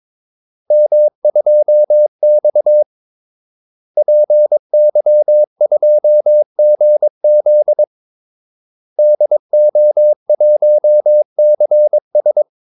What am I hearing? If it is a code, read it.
Morse code, heard as a keyed tone, about 22 wpm, 600 Hz. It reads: M2X PY2GZ DO1CS
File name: Callsigns-22wpm